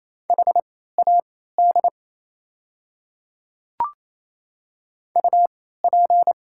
Morse code, heard as a keyed tone, about 28 wpm, spaced as for 16 wpm, 690 Hz. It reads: HAD UP
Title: 3x Character Spacing - Top 100 Words - 28wpm